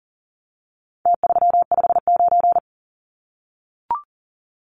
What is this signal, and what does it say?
Morse code, heard as a keyed tone, about 40 wpm, 700 Hz. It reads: T359
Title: Numbers - Contest Style - 4-Digit 40wpm